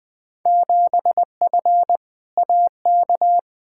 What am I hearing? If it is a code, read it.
Morse code, heard as a keyed tone, about 20 wpm, 700 Hz. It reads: 7F AK
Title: Field Day Exchanges Encore - 20wpm